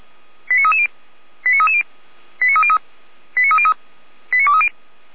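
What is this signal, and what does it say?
Five-tone selective call (ZVEI), heard as quick runs of five tones, tones of about 70 ms. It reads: 89330 89330 89303 89303 89230
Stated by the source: ZVEI five-tone selective call received on SDR, fire brigade Feuerwehr Orlamünde.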